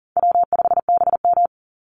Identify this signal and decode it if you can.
Morse code, heard as a keyed tone, about 40 wpm, 700 Hz. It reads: W5BM
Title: Easy Call Signs - 40wpm